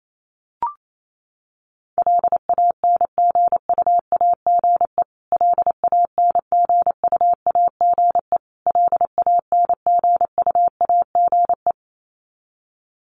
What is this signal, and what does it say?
Morse code, heard as a keyed tone, about 28 wpm, 700 Hz. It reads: LANGUAGE LANGUAGE LANGUAGE
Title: Top 500 Words Repeated 28wpm